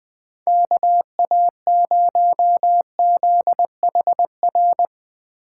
Morse code, heard as a keyed tone, about 20 wpm, 700 Hz. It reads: KA0ZHR
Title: Speed Racing - USA Call Signs 17wpm